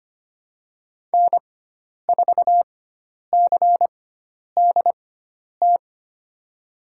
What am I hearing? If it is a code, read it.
Morse code, heard as a keyed tone, about 25 wpm, spaced as for 10 wpm, 700 Hz. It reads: N4CDT
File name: Lesson 16g - Letter c - Call Signs 25wpm